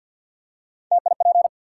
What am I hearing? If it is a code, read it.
Morse code, heard as a keyed tone, about 53 wpm, 700 Hz. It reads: TIP
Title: Three Letter Words - 50wpm